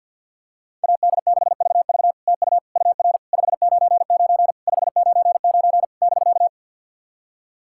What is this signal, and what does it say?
Morse code, heard as a keyed tone, about 46 wpm, 690 Hz. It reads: AD6VV TU UR 599 599 <BK>
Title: SOTA QSO Practice 50wpm